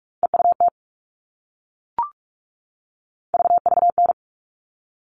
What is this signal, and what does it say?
Morse code, heard as a keyed tone, about 44 wpm, 700 Hz. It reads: EUT VVN
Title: All Cut Numbers - 3-Digit 45wpm